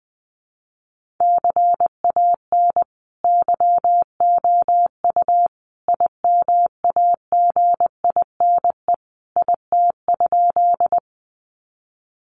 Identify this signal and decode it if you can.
Morse code, heard as a keyed tone, about 20 wpm, 700 Hz. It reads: CAN YOU IMAGINE IT?
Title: Sentences from Top 1400 Words 20wpm